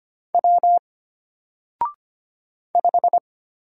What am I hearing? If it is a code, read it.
Morse code, heard as a keyed tone, about 25 wpm, 700 Hz. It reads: W 5
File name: Single-Letter-Number-25wpm